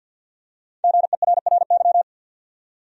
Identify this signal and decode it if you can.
Morse code, heard as a keyed tone, about 53 wpm, 700 Hz. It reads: GERRY
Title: USA Common Names - Max 5 Letters - 50wpm